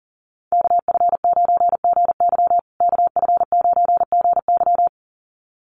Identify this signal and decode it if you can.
Morse code, heard as a keyed tone, about 37 wpm, 700 Hz. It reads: KF9GY KF9GY
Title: USA Call Signs Repeated 2x 40wpm